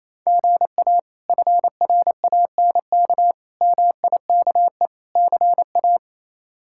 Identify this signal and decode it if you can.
Morse code, heard as a keyed tone, about 28 wpm, 700 Hz. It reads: GA FRANK MIKE CA
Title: K1USN SST QSO Practice - 28wpm